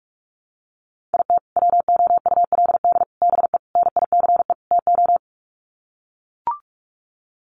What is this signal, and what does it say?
Morse code, heard as a keyed tone, about 44 wpm, 700 Hz. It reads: IT WOULD BE NICE TO
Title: Sets of 5 Words Encore2 45wpm